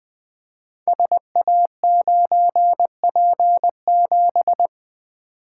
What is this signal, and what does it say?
Morse code, heard as a keyed tone, about 20 wpm, 700 Hz. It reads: SA9P7
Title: Code Groups - 5 Letter:Numbers - 20wpm